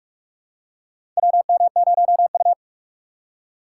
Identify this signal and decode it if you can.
Morse code, heard as a keyed tone, about 44 wpm, 700 Hz. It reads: WM0U
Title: USA Call Signs - 4 Characters - 6x Character Spacing - 45wpm